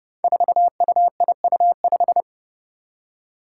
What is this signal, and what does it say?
Morse code, heard as a keyed tone, about 32 wpm, 700 Hz. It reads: 4UIU5
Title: Five Challenging Letter-Number Combinations 30wpm